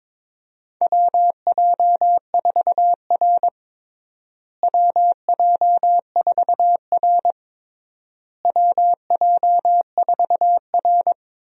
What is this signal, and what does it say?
Morse code, heard as a keyed tone, about 21 wpm, 700 Hz. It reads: WJ4R WJ4R WJ4R
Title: USA Call Signs - 4 Characters - Repeated 3x - 22wpm